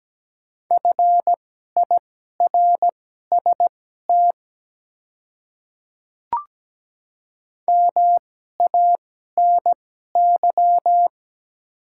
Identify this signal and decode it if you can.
Morse code, heard as a keyed tone, about 17 wpm, spaced as for 12 wpm, 690 Hz. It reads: FIRST MANY
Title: 2x Character Spacing - Top 100 Words - 17wpm